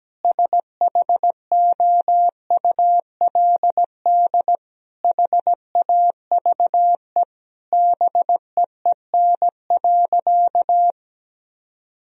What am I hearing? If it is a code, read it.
Morse code, heard as a keyed tone, about 17 wpm, 700 Hz. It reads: SHOULD HAVE BEEN.
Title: Sentences from Top 1000 Words - Review Part 4 - 17wpm